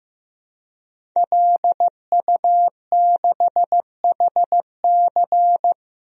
Morse code, heard as a keyed tone, about 15 wpm, 700 Hz. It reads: LU6HC
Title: DXCC-100 Call Signs - 5 Characters - Repeated 2x - 15wpm